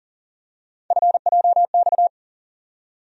Morse code, heard as a keyed tone, about 40 wpm, 700 Hz. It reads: FJX
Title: USA Call Sign Postfixes - 3 Letters 40wpm